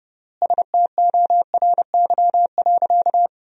Morse code, heard as a keyed tone, about 32 wpm, 700 Hz. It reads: STORY.
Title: 3x Word-Spacing - Sentences from Top 300 Words - 30wpm